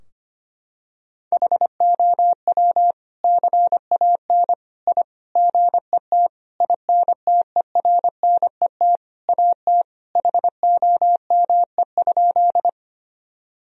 Morse code, heard as a keyed tone, about 25 wpm, 700 Hz. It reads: HOW CAN I GET INTERNET AT HOME?
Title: Sentences from Top 1000 Words - Review Part 4 - 25wpm